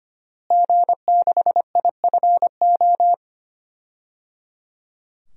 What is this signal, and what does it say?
Morse code, heard as a keyed tone, about 25 wpm, 700 Hz. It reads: G6IFO
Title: DXCC-100 Call Signs - 5 Characters - Repeated 3x - 25wpm